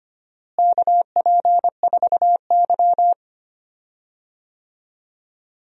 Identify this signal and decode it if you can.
Morse code, heard as a keyed tone, about 25 wpm, 700 Hz. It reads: KP4Y
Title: DXCC-100 Call Signs - 4 Characters - Repeated 2x - 25wpm